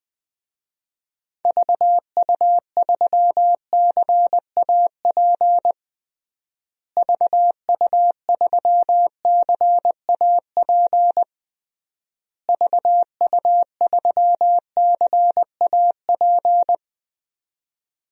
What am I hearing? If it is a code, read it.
Morse code, heard as a keyed tone, about 20 wpm, 700 Hz. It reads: VU3CAP VU3CAP VU3CAP
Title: DXCC-100 Call Signs - 6 Characters - Repeated 3x - 20wpm